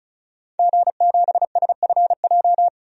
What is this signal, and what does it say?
Morse code, heard as a keyed tone, about 36 wpm, 700 Hz. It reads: G7SFJ
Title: DXCC-100 Call Signs - 5 Characters - 35wpm